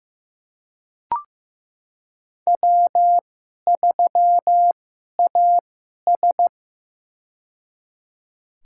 Morse code, heard as a keyed tone, about 15 wpm, spaced as for 11 wpm, 700 Hz. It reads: W3AS
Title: USA Call Signs - 4 Characters - 2x Character Spacing - 15wpm